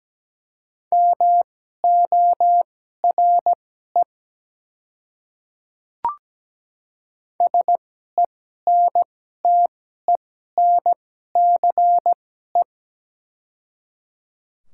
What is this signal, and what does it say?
Morse code, heard as a keyed tone, about 17 wpm, spaced as for 12 wpm, 700 Hz. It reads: MORE SENTENCE
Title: 2x Character Spacing - Top 200 Words - 17wpm